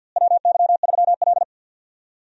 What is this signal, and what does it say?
Morse code, heard as a keyed tone, about 52 wpm, 700 Hz. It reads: WY3L
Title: USA Call Signs - 4 Characters - Repeated 3x - 50wpm